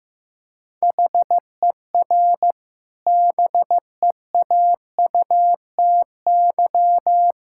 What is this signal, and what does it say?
Morse code, heard as a keyed tone, about 15 wpm, 700 Hz. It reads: HER BEAUTY
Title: Sentences from Top 1000 Words - Review Part 5 - 15wpm